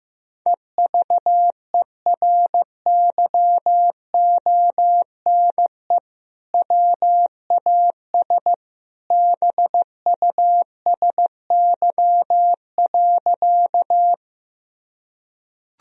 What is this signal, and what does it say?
Morse code, heard as a keyed tone, about 15 wpm, 700 Hz. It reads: EVERYONE WAS BUSY.
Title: Sentences from Top 1000 Words - Review Part 3 - 15wpm